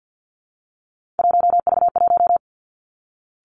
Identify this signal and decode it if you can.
Morse code, heard as a keyed tone, about 50 wpm, 700 Hz. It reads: 1V1
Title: Three Challenging Letter-Number Combinations 50wpm